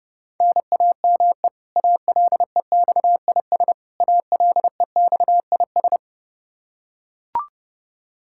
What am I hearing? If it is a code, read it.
Morse code, heard as a keyed tone, about 30 wpm, 700 Hz. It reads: NAME ALEXIS ALEXIS
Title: Practice QSO - Exchange 1 - Beginning - 30wpm